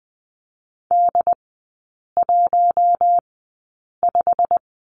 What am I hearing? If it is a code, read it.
Morse code, heard as a keyed tone, about 20 wpm, spaced as for 8 wpm, 700 Hz. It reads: D15
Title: 087 Lesson 15e - Number 5 - Three Characters 20wpm